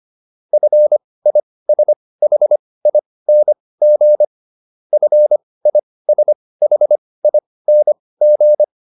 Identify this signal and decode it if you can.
Morse code, heard as a keyed tone, about 25 wpm, spaced as for 18 wpm, 600 Hz. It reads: FISHING FISHING